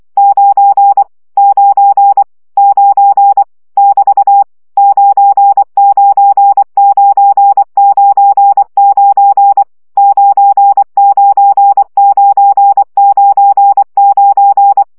Morse code, heard as a keyed tone, about 24 wpm, 800 Hz. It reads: 9 9 9 <BT> 99999 99999
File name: CW gyakorlas 9 20 baud